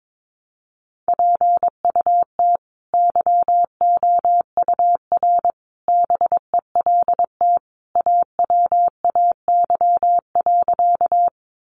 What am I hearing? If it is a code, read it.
Morse code, heard as a keyed tone, about 22 wpm, 700 Hz. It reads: PUT YOUR BELT AWAY.
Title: Sentences from Top 1400 Words 22wpm